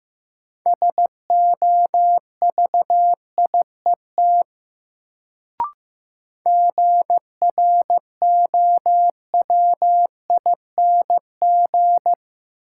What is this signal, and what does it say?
Morse code, heard as a keyed tone, about 15 wpm, 700 Hz. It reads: SOVIET GROWING
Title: Sets of 5 Words Encore - New Word Drill - 15wpm